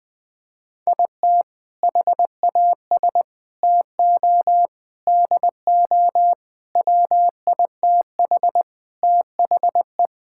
Morse code, heard as a keyed tone, about 20 wpm, 700 Hz. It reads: IT HAS TO DO WITH THE
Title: Sentences with 5-Word Phrases - 20wpm